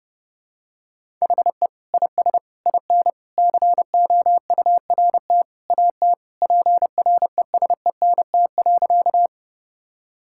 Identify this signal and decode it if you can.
Morse code, heard as a keyed tone, about 30 wpm, 700 Hz. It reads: HE IS IN COURT AT PRESENT.
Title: Sentences from Top 1600 Words 30wpm